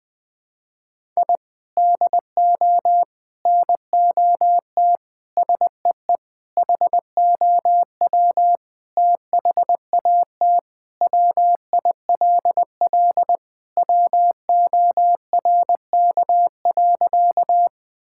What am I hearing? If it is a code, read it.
Morse code, heard as a keyed tone, about 20 wpm, 700 Hz. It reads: I DO NOT SEE HOW THAT WILL WORK.
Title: Sentences with 5-Word Phrases - Spoken Beginning - 20wpm